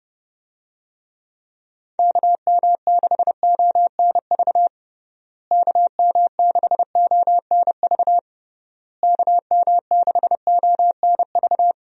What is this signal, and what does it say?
Morse code, heard as a keyed tone, about 30 wpm, 700 Hz. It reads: KM6ONV KM6ONV KM6ONV
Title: USA Call Signs - 6 Characters - Repeated 3x - 30wpm